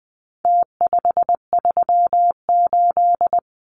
Morse code, heard as a keyed tone, about 20 wpm, 700 Hz. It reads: T538
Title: Numbers - Contest Style - 4-Digit 20wpm